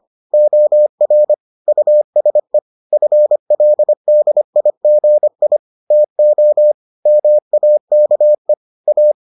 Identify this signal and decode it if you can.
Morse code, heard as a keyed tone, about 25 wpm, 600 Hz. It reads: OR USE FLDIGI TO MAKE A